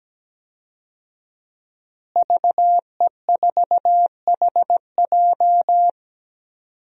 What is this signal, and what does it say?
Morse code, heard as a keyed tone, about 17 wpm, 700 Hz. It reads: VE4HJ
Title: DXCC-100 Call Signs - 5 Characters - Repeated 2x - 17wpm